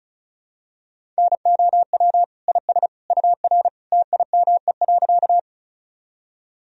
Morse code, heard as a keyed tone, about 34 wpm, 700 Hz. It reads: NOW IS UR TIME.
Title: Sentences from Top 100 Words and Abbreviations 35wpm